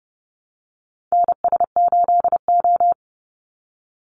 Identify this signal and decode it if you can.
Morse code, heard as a keyed tone, about 28 wpm, 700 Hz. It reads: NS8O
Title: Call Signs Repeated 2x - 30wpm